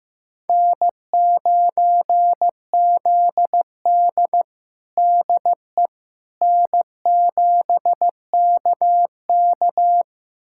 Morse code, heard as a keyed tone, about 15 wpm, 700 Hz. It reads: N9ZD DE N7KK
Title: USA Callsign DE Callsign - 4 Characters - 15wpm